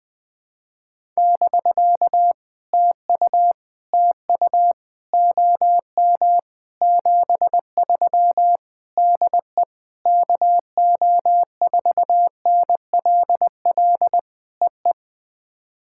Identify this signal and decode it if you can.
Morse code, heard as a keyed tone, about 20 wpm, 700 Hz. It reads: <BK> TU TU OM 73 DE KO4NLL EE
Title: POTA QSO Practice - 20wpm